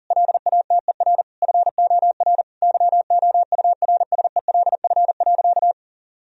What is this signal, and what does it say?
Morse code, heard as a keyed tone, about 40 wpm, 700 Hz. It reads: LATER FOR YOURSELF.
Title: Sentences from Top 1000 Words - Review Part 4 - 40wpm